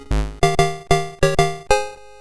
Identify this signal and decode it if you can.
There is no code.